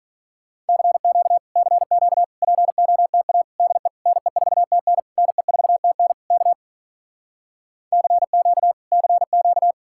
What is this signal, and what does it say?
Morse code, heard as a keyed tone, about 46 wpm, 700 Hz. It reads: CQ CQ POTA DE NE4TN NE4TN K CQ CQ